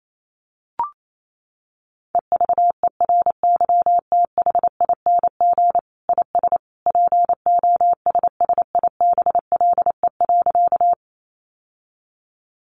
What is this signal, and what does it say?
Morse code, heard as a keyed tone, about 28 wpm, 700 Hz. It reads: EVERYTHING IS POSSIBLE.
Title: Sentences from Top 700 Words 28wpm